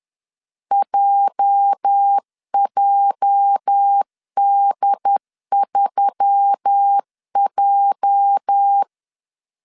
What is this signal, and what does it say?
Morse code, heard as a keyed tone, about 10 wpm, 800 Hz. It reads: JJD3J
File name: LEZIONI / 8 YJ45 / 8 YJ 10wpm 2651